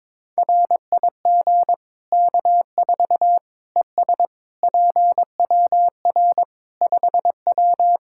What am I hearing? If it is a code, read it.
Morse code, heard as a keyed tone, about 21 wpm, 700 Hz. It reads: RIG K4 ES PWR 5W
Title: K1USN SST QSO Practice 22wpm